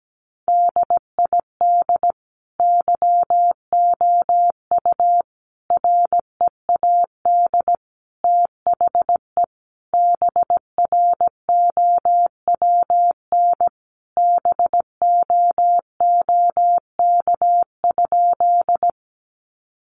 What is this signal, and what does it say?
Morse code, heard as a keyed tone, about 17 wpm, 700 Hz. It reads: DID YOU READ THE BROWN BOOK?
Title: Sentences from Top 1200 Words 17wpm